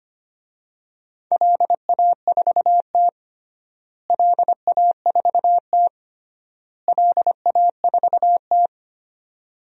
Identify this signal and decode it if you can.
Morse code, heard as a keyed tone, about 25 wpm, 700 Hz. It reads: LA4T LA4T LA4T
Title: DXCC-100 Call Signs - 4 Characters - Repeated 3x - 25wpm